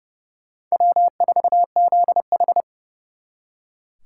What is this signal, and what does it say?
Morse code, heard as a keyed tone, about 37 wpm, 700 Hz. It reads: W4ZH
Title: Speed Racing - USA 1x2 Call Signs 30wpm